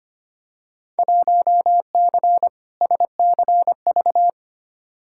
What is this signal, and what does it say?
Morse code, heard as a keyed tone, about 25 wpm, 700 Hz. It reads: 1C SCV
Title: Field Day Exchanges - 25wpm